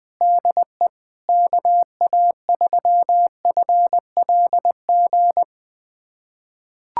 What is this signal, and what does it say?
Morse code, heard as a keyed tone, about 20 wpm, 700 Hz. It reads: DE KA3FLG
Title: USA Callsign DE Callsign - 6 Characters - 20wpm